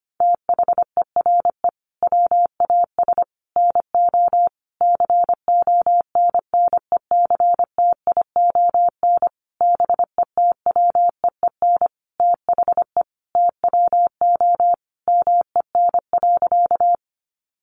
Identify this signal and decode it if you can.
Morse code, heard as a keyed tone, about 25 wpm, 700 Hz. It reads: THERE WAS NO CONNECTION BETWEEN THE TWO MEN.
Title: Sentences from Top 1300 Words 25wpm